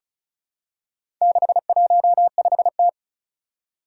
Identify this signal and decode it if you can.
Morse code, heard as a keyed tone, about 34 wpm, 700 Hz. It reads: 615T
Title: Four Challenging Letter-Number Combinations 35wpm